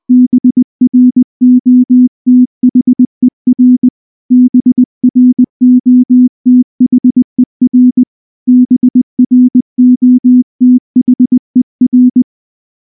Morse code, heard as a keyed tone, about 19 wpm, 250 Hz. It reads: BROTHER BROTHER BROTHER
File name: morse / morse-training-20WPM-5